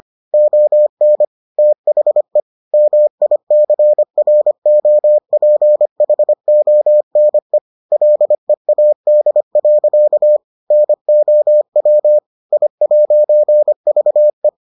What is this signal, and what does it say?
Morse code, heard as a keyed tone, about 25 wpm, 600 Hz. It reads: ON THE MICROPHONE LEAD. NOW I'VE